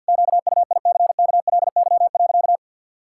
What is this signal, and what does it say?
Morse code, heard as a keyed tone, about 50 wpm, 700 Hz. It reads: QUICKLY.
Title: Pangram Challenge v2 - 50wpm